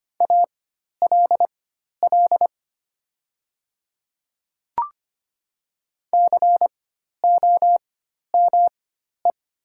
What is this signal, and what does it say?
Morse code, heard as a keyed tone, about 25 wpm, spaced as for 12 wpm, 700 Hz. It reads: ALL COME
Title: 4x Character Spacing - Top 100 Words - 25wpm